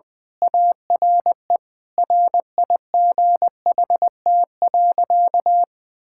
Morse code, heard as a keyed tone, about 20 wpm, 700 Hz. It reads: ARE RIGHT.
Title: Sentences with 3-Word Phrases - Spoken Beginning - 20wpm